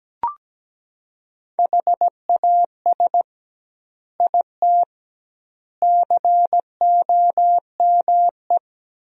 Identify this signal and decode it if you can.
Morse code, heard as a keyed tone, about 17 wpm, 700 Hz. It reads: HAS IT COME
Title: 2x Word-Spacing - Sentences from Top 100 Words - 17wpm